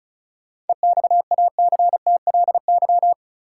Morse code, heard as a keyed tone, about 36 wpm, 700 Hz. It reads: EXACTLY
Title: Seven Letter Words - 35wpm